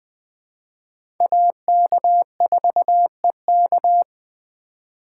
Morse code, heard as a keyed tone, about 20 wpm, 700 Hz. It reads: AK4EK
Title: USA Call Signs - 5 Characters - 3x Character Spacing - 20wpm